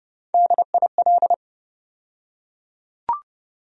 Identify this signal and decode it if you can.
Morse code, heard as a keyed tone, about 32 wpm, 700 Hz. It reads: DIL
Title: USA Call Sign Postfixes - 3 Letters 30wpm